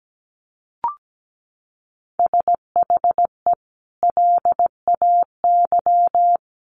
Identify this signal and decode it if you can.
Morse code, heard as a keyed tone, about 17 wpm, 700 Hz. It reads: SHE LAY
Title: Sentences from Top 900 Words 17wpm